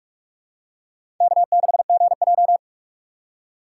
Morse code, heard as a keyed tone, about 41 wpm, 700 Hz. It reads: K6GJ
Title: USA Call Signs - 4 Characters - 6x Character Spacing - 45wpm